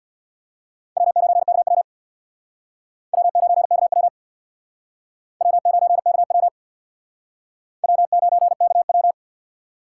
Morse code, heard as a keyed tone, about 62 wpm, 700 Hz. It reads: W9KW W9KW W9KW W9KW
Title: Speed Racing - USA 1x2 Call Signs 50wpm